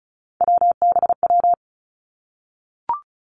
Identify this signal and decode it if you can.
Morse code, heard as a keyed tone, about 36 wpm, 700 Hz. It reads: WBW
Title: Three Challenging Letter-Number Combinations 35wpm